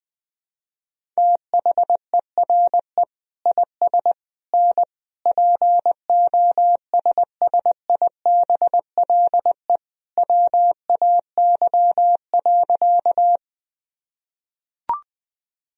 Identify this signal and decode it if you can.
Morse code, heard as a keyed tone, about 20 wpm, 700 Hz. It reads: THERE IS N POSSIBLE WAY.
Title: Sentences from Top 500 Words and Abbreviations 20wpm